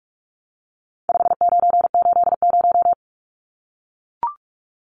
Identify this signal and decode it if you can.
Morse code, heard as a keyed tone, about 44 wpm, 700 Hz. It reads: 5980
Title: Numbers - 4-Digit 45wpm